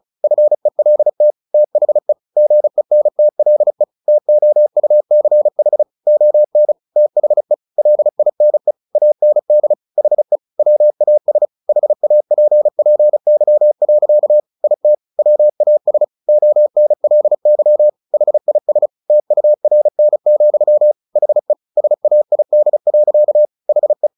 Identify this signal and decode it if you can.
Morse code, heard as a keyed tone, about 34 wpm, 600 Hz. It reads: FELT THE GENTLE TOUCH ON THE LINE AND HE WAS HAPPY. IT WAS ONLY HIS TURN, HE SAID. HE
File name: the old man and the sea / the old man and the sea 1 35wpm